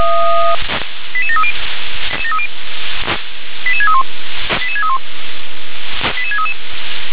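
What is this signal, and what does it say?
Five-tone selective call (ZVEI), heard as quick runs of five tones, tones of about 70 ms. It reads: 88730 88631 88631 88633
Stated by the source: ZVEI five-tone call received on an SDR.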